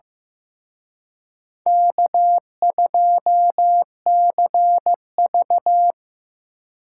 Morse code, heard as a keyed tone, about 17 wpm, 700 Hz. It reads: K2CV
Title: Speed Racing - USA 1x2 Call Signs 15wpm